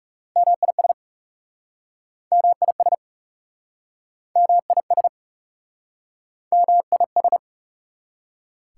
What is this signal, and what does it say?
Morse code, heard as a keyed tone, about 37 wpm, 700 Hz. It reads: MIS MIS MIS MIS
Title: Speed-Racing Prefixes and Suffixes 30wpm